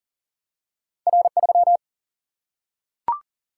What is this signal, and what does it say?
Morse code, heard as a keyed tone, about 40 wpm, 700 Hz. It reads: R3
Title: Single-Letter Single-Number 40wpm